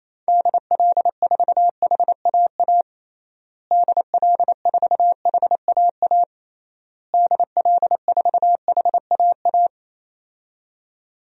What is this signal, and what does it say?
Morse code, heard as a keyed tone, about 27 wpm, 700 Hz. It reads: DL4HAA DL4HAA DL4HAA
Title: DXCC-100 Call Signs - 6 Characters - Repeated 3x - 28wpm